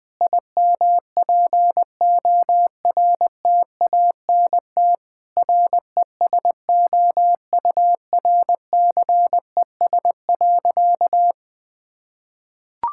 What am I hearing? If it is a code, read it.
Morse code, heard as a keyed tone, about 21 wpm, 700 Hz. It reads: IMPORTANT RESOURCES.
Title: Sentences with 5-Word Phrases - Spoken Beginning - 20wpm